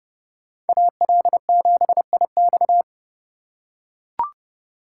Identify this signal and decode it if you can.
Morse code, heard as a keyed tone, about 29 wpm, 700 Hz. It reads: AL7IX